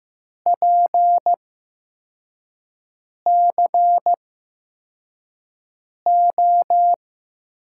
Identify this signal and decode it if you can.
Morse code, heard as a keyed tone, about 15 wpm, spaced as for 4 wpm, 700 Hz. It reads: PCO
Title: USA Call Signs - 6 Characters - 8x Character Spacing - 15wpm